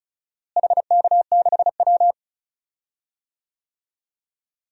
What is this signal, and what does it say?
Morse code, heard as a keyed tone, about 34 wpm, 700 Hz. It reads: HK6W
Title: DXCC-100 Call Signs - 4 Characters - Repeated 2x - 35wpm